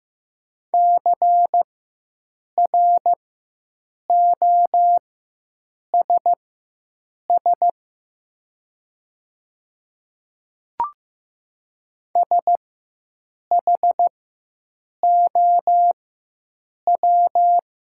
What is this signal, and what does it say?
Morse code, heard as a keyed tone, about 15 wpm, spaced as for 7 wpm, 700 Hz. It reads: CROSS SHOW
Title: 4x Character Spacing - Top 500 Words - 15wpm